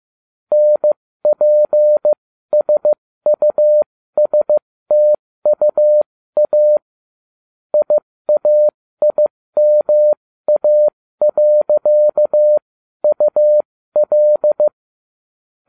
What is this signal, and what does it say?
Morse code, heard as a keyed tone, about 15 wpm, spaced as for 13 wpm, 600 Hz. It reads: NPSUSTUA IAIMA.UL